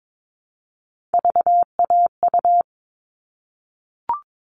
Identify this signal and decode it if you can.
Morse code, heard as a keyed tone, about 22 wpm, 700 Hz. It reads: VAU
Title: All Cut Numbers - 3-Digit 22wpm